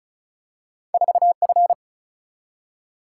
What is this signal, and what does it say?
Morse code, heard as a keyed tone, about 36 wpm, 700 Hz. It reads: 4F
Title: Single-Letter Single-Number 35wpm